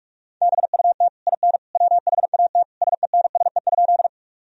Morse code, heard as a keyed tone, about 46 wpm, 700 Hz. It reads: BUT IN WHAT SENSE?
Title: Sentences from Top 1000 Words - Review Part 4 - 45wpm